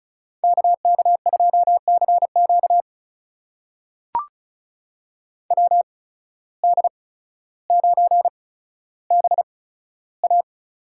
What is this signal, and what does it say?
Morse code, heard as a keyed tone, about 36 wpm, spaced as for 15 wpm, 700 Hz. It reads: KK2CQ WD9BA
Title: USA Call Signs - 8x Character Spacing - 35wpm